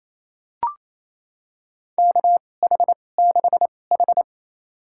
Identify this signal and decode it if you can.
Morse code, heard as a keyed tone, about 28 wpm, spaced as for 20 wpm, 700 Hz. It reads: KH6H
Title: USA Call Signs - 4 Characters - 2x Character Spacing - 28wpm